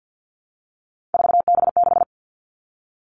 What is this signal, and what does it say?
Morse code, heard as a keyed tone, about 50 wpm, 700 Hz. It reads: 4B6
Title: Three Challenging Letter-Number Combinations 50wpm